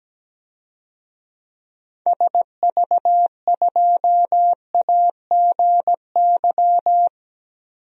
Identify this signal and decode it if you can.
Morse code, heard as a keyed tone, about 17 wpm, 700 Hz. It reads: SV2AGY
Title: DXCC-100 Call Signs - 6 Characters - Repeated 2x - 17wpm